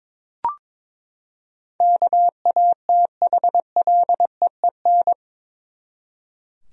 Morse code, heard as a keyed tone, about 22 wpm, 700 Hz. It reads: KATHLEEN
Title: USA Common Names - 22wpm